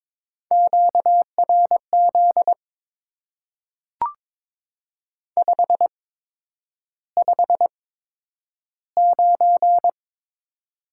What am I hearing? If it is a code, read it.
Morse code, heard as a keyed tone, about 21 wpm, spaced as for 9 wpm, 700 Hz. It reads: QRZ 559
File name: Top 90 QSO Elements - 8x Character Spacing - 22wpm